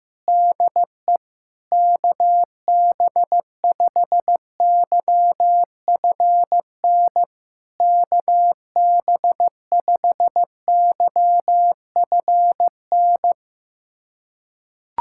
Morse code, heard as a keyed tone, about 15 wpm, 700 Hz. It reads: DE KB5YFN KB5YFN
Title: SOTA QSO Practice 15wpm